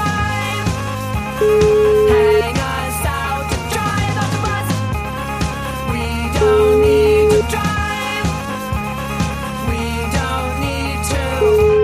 Ringback tone (Europe, 425 Hz), with noise.